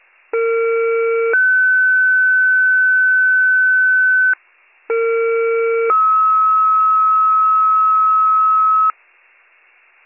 Two-tone page, heard as a one-second tone, then a three-second tone, about 460 then 1500 Hz, about 460 then 1290 Hz, with noise. It is heard from 0.3 to 8.9 s.